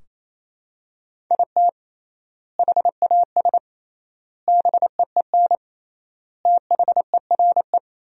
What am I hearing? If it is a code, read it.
Morse code, heard as a keyed tone, about 28 wpm, 700 Hz. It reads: IT HAS BEEN THERE
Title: 3x Word-Spacing - Sentences from Top 400 Words - 28wpm